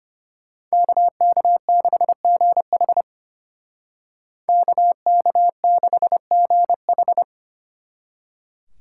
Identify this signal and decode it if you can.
Morse code, heard as a keyed tone, about 30 wpm, 700 Hz. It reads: KK6GH KK6GH
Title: Speed Racing - USA 2x2 Call Signs 25wpm